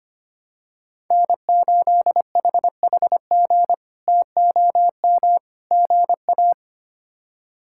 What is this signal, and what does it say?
Morse code, heard as a keyed tone, about 25 wpm, 700 Hz. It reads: N8HHG TOM GA
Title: K1USN SST QSO Practice - 25wpm